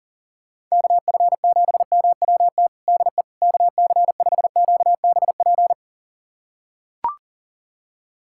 Morse code, heard as a keyed tone, about 40 wpm, 700 Hz. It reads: KF7MWT DE KC5QBP
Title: USA Callsign DE Callsign - 6 Characters - 40wpm